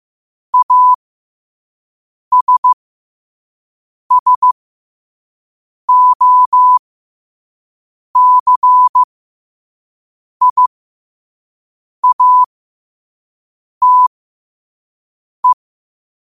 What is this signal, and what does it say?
Morse code, heard as a keyed tone, about 15 wpm, spaced as for 5 wpm, 1000 Hz. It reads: ASSOCIATE
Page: New quotes every day in morse code at 5 Words per minute.